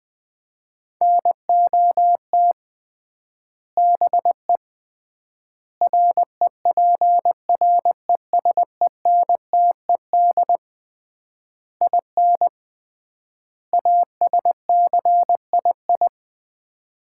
Morse code, heard as a keyed tone, about 20 wpm, 700 Hz. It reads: NOT BE REPRESENTED IN ASCII
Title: 3x Word-Spacing - Sets of 5 Words - 20wpm